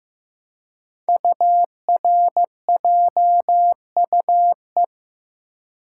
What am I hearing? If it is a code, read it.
Morse code, heard as a keyed tone, about 15 wpm, 700 Hz. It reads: URJUE
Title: Code Groups - 5 Letters - 15wpm